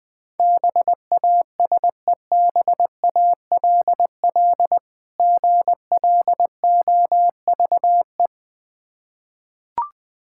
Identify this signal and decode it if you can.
Morse code, heard as a keyed tone, about 21 wpm, 700 Hz. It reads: BASEBALL GLOVE
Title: Difficult Words with Hints - 20wpm